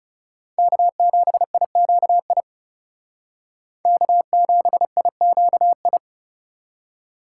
Morse code, heard as a keyed tone, about 36 wpm, 700 Hz. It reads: K7IQI K7IQI
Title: Speed Racing - USA Call Signs 30wpm